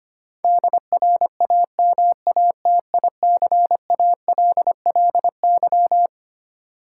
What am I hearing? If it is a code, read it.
Morse code, heard as a keyed tone, about 25 wpm, 700 Hz. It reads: DRAMATICALLY
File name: Twelve Letter Words - 25wpm